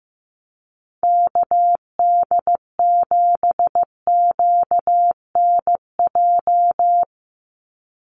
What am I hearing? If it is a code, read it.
Morse code, heard as a keyed tone, about 15 wpm, 700 Hz. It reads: KD7QNJ
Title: USA Call Signs 15wpm